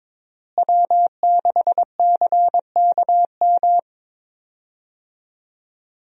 Morse code, heard as a keyed tone, about 21 wpm, 700 Hz. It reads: W6CKM
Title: USA Call Signs - 5 Characters - Repeated 3x - 22wpm